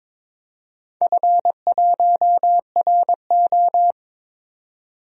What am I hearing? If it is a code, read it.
Morse code, heard as a keyed tone, about 22 wpm, 700 Hz. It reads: F1RO
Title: DXCC-100 Call Signs - 4 Characters - 22wpm